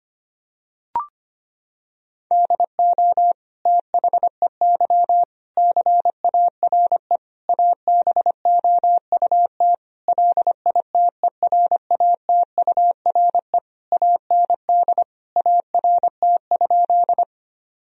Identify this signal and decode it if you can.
Morse code, heard as a keyed tone, about 25 wpm, 700 Hz. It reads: DO THEY CARE ABOUT LITERATURE AND ART?
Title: Sentences from Top 1000 Words - Review Part 3 - 25wpm